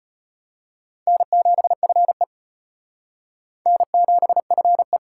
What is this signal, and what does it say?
Morse code, heard as a keyed tone, about 32 wpm, 700 Hz. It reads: N7FE N7FE